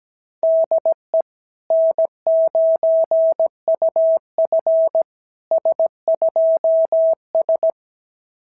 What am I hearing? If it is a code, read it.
Morse code, heard as a keyed tone, about 17 wpm, 660 Hz. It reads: DE N9UF S2S
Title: SOTA QSO Practice 17wpm